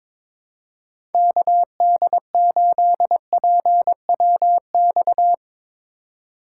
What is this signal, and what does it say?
Morse code, heard as a keyed tone, about 22 wpm, 700 Hz. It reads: KD8PWX
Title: USA Call Signs - 6 Characters - 2x Character Spacing - 22wpm